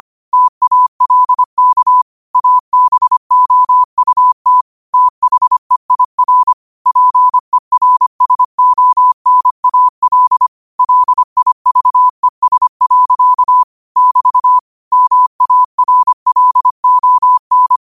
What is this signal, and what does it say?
Morse code, heard as a keyed tone, about 25 wpm, 1000 Hz. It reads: TALK ABOUT THEIR PERSONAL LIVES. <BT> MARLON
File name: qotdfiles / 25wpm